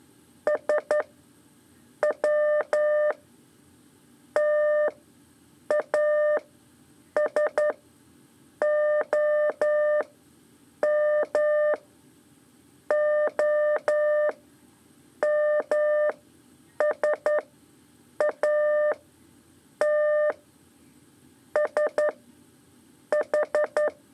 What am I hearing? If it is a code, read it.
Morse code, heard as a keyed tone, about 11 wpm, 600 Hz. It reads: S W T A S O M O M S A T S H
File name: 3 lezione CW 2 AIR